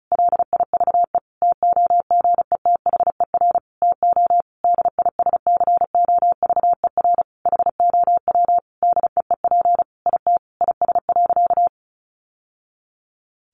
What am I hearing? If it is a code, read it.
Morse code, heard as a keyed tone, about 34 wpm, 700 Hz. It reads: LIVE TOGETHER TO DISCOVER HOW DEEP IT IS.
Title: Sentences from Top 1100 Words 35wpm